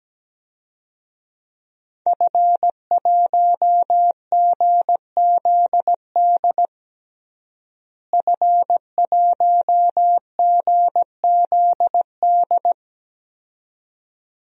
Morse code, heard as a keyed tone, about 17 wpm, 700 Hz. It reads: F1GZD F1GZD
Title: DXCC-100 Call Signs - 5 Characters - Repeated 2x - 17wpm